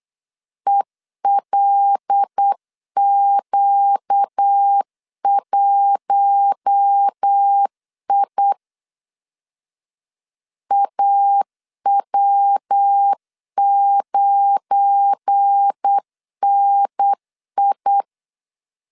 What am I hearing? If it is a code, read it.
Morse code, heard as a keyed tone, about 8 wpm, 800 Hz. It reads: ELQ1I AW9NI
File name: LEZIONI / 7 GF9Z / 7 GF9Z 8wpm 3213